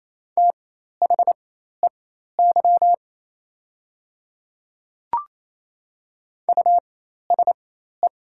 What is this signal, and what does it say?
Morse code, heard as a keyed tone, about 28 wpm, spaced as for 13 wpm, 700 Hz. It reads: THEY USE
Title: 4x Character Spacing - Top 200 Words - 28wpm